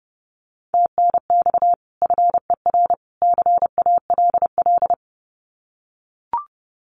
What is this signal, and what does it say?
Morse code, heard as a keyed tone, about 30 wpm, 700 Hz. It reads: TNX FER CALL
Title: Common QSO Phrases 30wpm